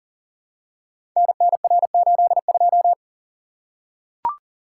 Morse code, heard as a keyed tone, about 42 wpm, 700 Hz. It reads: NNR82
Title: Code Groups - 5 Letter:Numbers - 40wpm